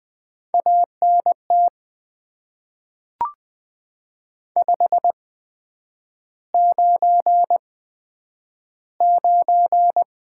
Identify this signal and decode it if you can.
Morse code, heard as a keyed tone, about 20 wpm, spaced as for 9 wpm, 700 Hz. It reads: ANT 599
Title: Top 90 QSO Elements - 8x Character Spacing - 20wpm